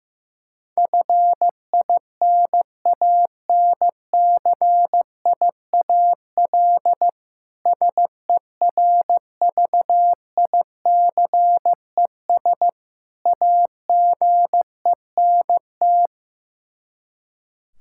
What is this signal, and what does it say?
Morse code, heard as a keyed tone, about 15 wpm, 700 Hz. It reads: FINANCIAL SERVICES AGENT
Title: K1USN SST QSO Practice 15wpm